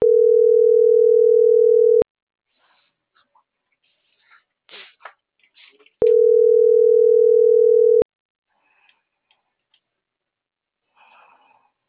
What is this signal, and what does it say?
Ringback tone (North America), 440 + 480 Hz.